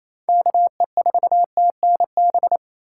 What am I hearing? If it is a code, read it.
Morse code, heard as a keyed tone, about 28 wpm, 700 Hz. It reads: KE4TNB
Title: USA Call Signs - 4x Character Spacing - 28wpm